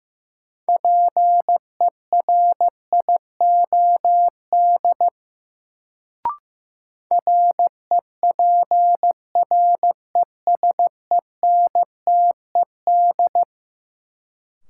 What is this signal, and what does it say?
Morse code, heard as a keyed tone, about 15 wpm, 700 Hz. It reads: PERIOD REPRESENTED
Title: Sets of 5 Words - New Word Drill - 15wpm